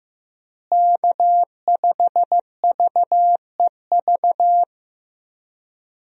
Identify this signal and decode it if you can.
Morse code, heard as a keyed tone, about 15 wpm, 700 Hz. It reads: K5VEV
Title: USA Call Signs - 5 Characters - Repeated 3x - 15wpm